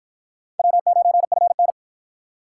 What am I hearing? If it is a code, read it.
Morse code, heard as a keyed tone, about 48 wpm, 700 Hz. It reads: W9FN